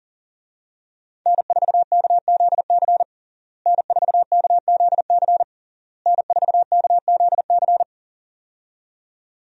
Morse code, heard as a keyed tone, about 40 wpm, 700 Hz. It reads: N4KZC N4KZC N4KZC
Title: USA Call Signs - 5 Characters - Repeated 3x - 40wpm